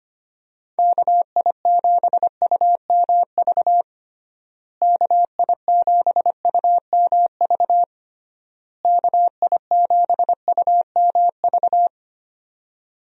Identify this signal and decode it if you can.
Morse code, heard as a keyed tone, about 25 wpm, 700 Hz. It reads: KI7UMV KI7UMV KI7UMV
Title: USA Call Signs - 6 Characters - Repeated 3x - 25wpm